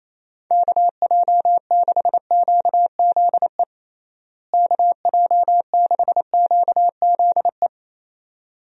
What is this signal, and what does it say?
Morse code, heard as a keyed tone, about 28 wpm, 700 Hz. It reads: KJ6QZE KJ6QZE
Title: USA Call Signs - 6 Characters - Repeated 2x - 28wpm